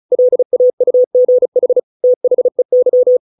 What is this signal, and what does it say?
Morse code, heard as a keyed tone, about 34 wpm, 500 Hz. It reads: LAUGH THEY
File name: 632 words / 632 Words 35WPM